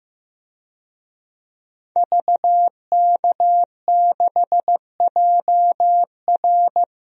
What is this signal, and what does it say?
Morse code, heard as a keyed tone, about 15 wpm, 700 Hz. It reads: VK6JR
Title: DXCC-100 Call Signs - 5 Characters - Repeated 2x - 15wpm